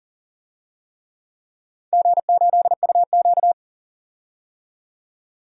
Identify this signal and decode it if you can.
Morse code, heard as a keyed tone, about 38 wpm, 700 Hz. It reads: G8UQ
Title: DXCC-100 Call Signs - 4 Characters - 40wpm